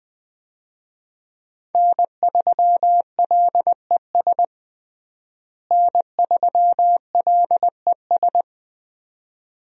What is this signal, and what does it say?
Morse code, heard as a keyed tone, about 20 wpm, 700 Hz. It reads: N3LES N3LES
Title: USA Call Signs - 5 Characters - Repeated 2x - 20wpm